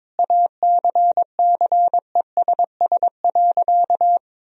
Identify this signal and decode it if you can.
Morse code, heard as a keyed tone, about 22 wpm, 700 Hz. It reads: ACCESS.
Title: Sentences from Top 1000 Words - Review Part 4 - 22wpm